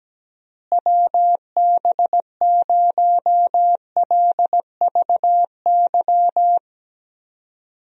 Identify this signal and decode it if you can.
Morse code, heard as a keyed tone, about 17 wpm, 700 Hz. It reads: WB0LVY
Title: USA Call Signs - 6 Characters - Repeated 2x - 17wpm